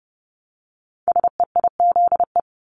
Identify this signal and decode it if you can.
Morse code, heard as a keyed tone, about 30 wpm, 700 Hz. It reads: SEIZE
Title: Multi-letter Phonemes Repeated 3x 30wpm